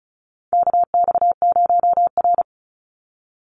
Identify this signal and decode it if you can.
Morse code, heard as a keyed tone, about 34 wpm, 700 Hz. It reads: KX0R
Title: NA SOTA Activator Callsigns 35wpm